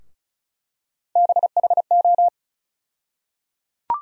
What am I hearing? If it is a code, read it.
Morse code, heard as a keyed tone, about 36 wpm, 700 Hz. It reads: BHO
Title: USA Call Sign Postfixes - 3 Letters 35wpm